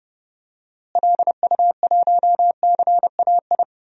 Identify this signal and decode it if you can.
Morse code, heard as a keyed tone, about 30 wpm, 700 Hz. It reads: LU1CAI
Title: DXCC-100 Call Signs - 6 Characters - Repeated 3x - 30wpm